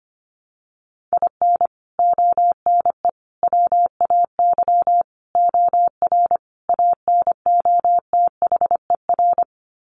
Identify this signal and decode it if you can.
Morse code, heard as a keyed tone, about 25 wpm, 700 Hz. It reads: IN ONE WAY OR ANOTHER
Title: Sets of 5 Words Encore 25wpm